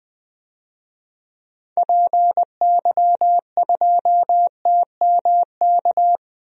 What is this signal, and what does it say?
Morse code, heard as a keyed tone, about 20 wpm, 700 Hz. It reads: PY2TMK
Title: DXCC-100 Call Signs - 6 Characters - 20wpm